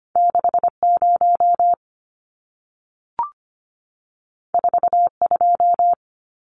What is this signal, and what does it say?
Morse code, heard as a keyed tone, about 25 wpm, 700 Hz. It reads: 60 42
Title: Numbers - 2-Digit 25wpm